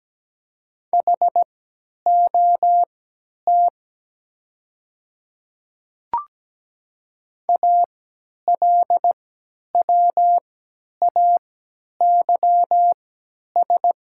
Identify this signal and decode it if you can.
Morse code, heard as a keyed tone, about 17 wpm, spaced as for 10 wpm, 700 Hz. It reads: HOT ALWAYS
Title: 3x Character Spacing - Top 500 Words - 17wpm